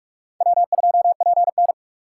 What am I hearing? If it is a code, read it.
Morse code, heard as a keyed tone, about 43 wpm, 700 Hz. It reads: W2PN